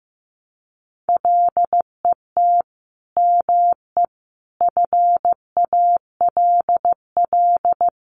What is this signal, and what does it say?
Morse code, heard as a keyed tone, about 15 wpm, 700 Hz. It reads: LET ME FALL
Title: Sentences from Top 1100 Words 15wpm